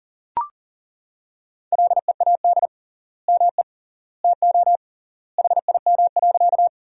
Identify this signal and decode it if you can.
Morse code, heard as a keyed tone, about 40 wpm, 700 Hz. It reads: LEAD ME TO HIM.
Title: 3x Word-Spacing - Sentences from Top 500 Words - 40wpm